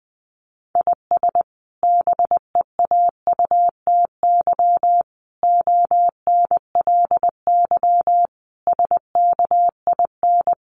Morse code, heard as a keyed tone, about 20 wpm, 700 Hz. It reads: IS BEAUTY ONLY SKIN
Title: Sentences from Top 1300 Words 20wpm